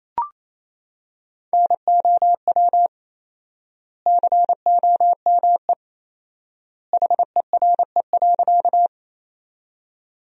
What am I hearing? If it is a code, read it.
Morse code, heard as a keyed tone, about 28 wpm, 700 Hz. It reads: NOW COME HERE.
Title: 4x Word-Spacing - Sentences from Top 200 Words - 28wpm